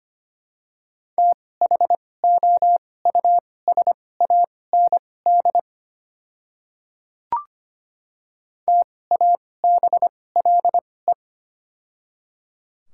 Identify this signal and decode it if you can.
Morse code, heard as a keyed tone, about 25 wpm, spaced as for 18 wpm, 700 Hz. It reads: THOUSAND TABLE
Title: 2x Character Spacing - Top 500 Words - 25wpm